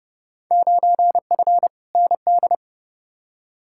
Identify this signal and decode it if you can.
Morse code, heard as a keyed tone, about 32 wpm, 700 Hz. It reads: 9F ND
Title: Field Day Exchanges Encore - 30wpm